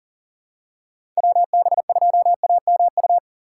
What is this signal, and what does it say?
Morse code, heard as a keyed tone, about 40 wpm, 700 Hz. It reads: WB2AMU
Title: USA Call Signs - 6 Characters - 8x Character Spacing - 40wpm